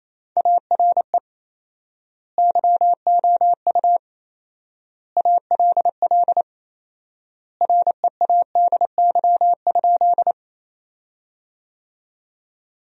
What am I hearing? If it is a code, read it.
Morse code, heard as a keyed tone, about 28 wpm, 700 Hz. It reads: ARE YOU ALL READY?
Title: 4x Word-Spacing - Sentences from Top 400 Words - 28wpm